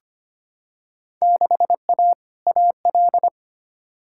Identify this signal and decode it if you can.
Morse code, heard as a keyed tone, about 25 wpm, 700 Hz. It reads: 6A AL
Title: Field Day Exchanges Encore - 25wpm